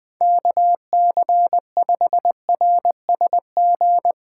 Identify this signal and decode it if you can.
Morse code, heard as a keyed tone, about 20 wpm, 700 Hz. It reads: KC5RSG
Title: Speed Racing - USA 2x3 Call Signs 20wpm